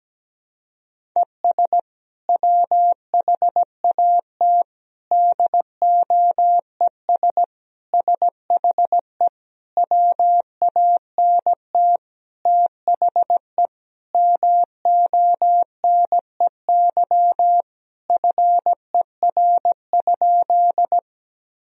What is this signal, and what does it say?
Morse code, heard as a keyed tone, about 17 wpm, 700 Hz. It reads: ES WHAT DOES SHE WANT THE MONEY FER?
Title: Sentences from Top 400 Words and Abbreviations 17wpm